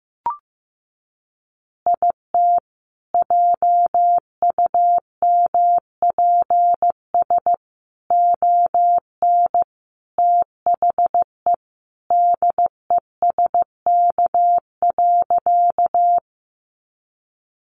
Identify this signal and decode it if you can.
Morse code, heard as a keyed tone, about 15 wpm, 700 Hz. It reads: IT JUMPS ON THE DESK.
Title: Additional Easy Phrases 15wpm